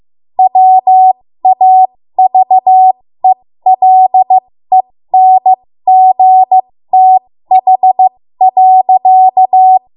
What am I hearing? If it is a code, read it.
Morse code, heard as a keyed tone, about 15 wpm, spaced as for 13 wpm, 750 Hz. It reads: WAVELENGTH.
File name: Morse / 13WPM